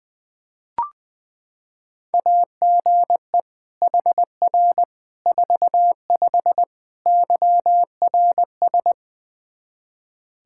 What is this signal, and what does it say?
Morse code, heard as a keyed tone, about 21 wpm, 700 Hz. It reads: AGE HR 45 YRS
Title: Practice QSO - Exchange 3 - More Info - 20wpm